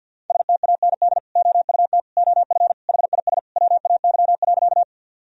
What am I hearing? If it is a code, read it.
Morse code, heard as a keyed tone, about 52 wpm, 700 Hz. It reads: STAND OUT OF HIS WAY.
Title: Sentences from Top 1000 Words - Review Part 1 - 50wpm